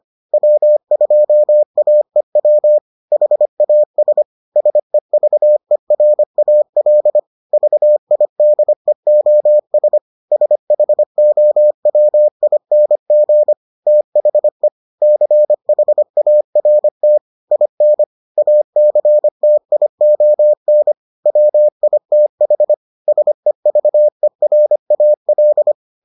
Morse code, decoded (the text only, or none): W2AEW HAS SEVERAL VIDEOS SHOWING THE CHART IN ACTION WITH SEVERAL